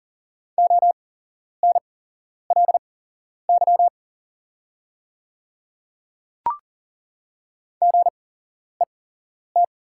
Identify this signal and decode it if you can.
Morse code, heard as a keyed tone, about 41 wpm, spaced as for 11 wpm, 690 Hz. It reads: ONLY GET
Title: 8x Character Spacing - Top 300 Words - 40wpm